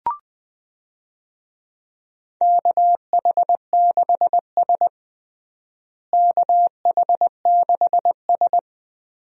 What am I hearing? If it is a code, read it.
Morse code, heard as a keyed tone, about 19 wpm, 700 Hz. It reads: KH6S KH6S
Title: DXCC-100 Call Signs - 4 Characters - Repeated 3x - 20wpm